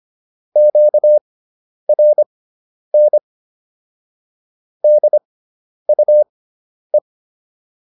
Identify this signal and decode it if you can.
Morse code, heard as a keyed tone, about 25 wpm, spaced as for 10 wpm, 600 Hz. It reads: QRN DUE